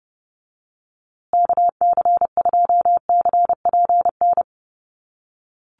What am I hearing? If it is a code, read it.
Morse code, heard as a keyed tone, about 30 wpm, 700 Hz. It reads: KC2CPN
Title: USA Call Signs Repeated 3x 30wpm